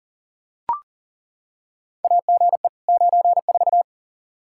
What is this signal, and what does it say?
Morse code, heard as a keyed tone, about 40 wpm, 700 Hz. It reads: AGE 94
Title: Practice QSO - Exchange 3 - More Info - 40wpm